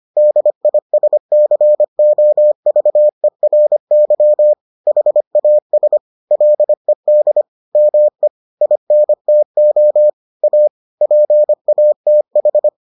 Morse code, heard as a keyed tone, about 25 wpm, 600 Hz. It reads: DISCOVERY HAS LED ME INTO A PATH